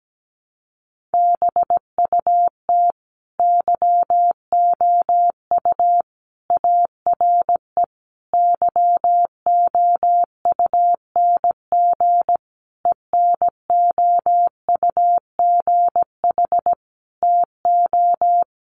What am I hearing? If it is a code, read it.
Morse code, heard as a keyed tone, about 17 wpm, 700 Hz. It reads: BUT YOU ARE YOUNG ENOUGH TO
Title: Sentences from Top 1200 Words 17wpm